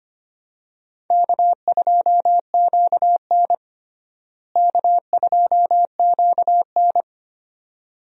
Morse code, heard as a keyed tone, about 25 wpm, 700 Hz. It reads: K2QN K2QN
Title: USA Call Signs - 4 Characters - Repeated 2x - 25wpm